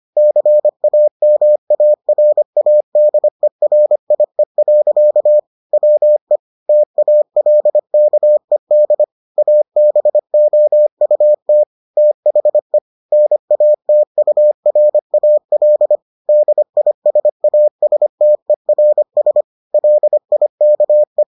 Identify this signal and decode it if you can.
Morse code, heard as a keyed tone, about 25 wpm, 600 Hz. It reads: CAMARADERIE. WE TALKED ABOUT THE NATURAL DISASTERS LIKE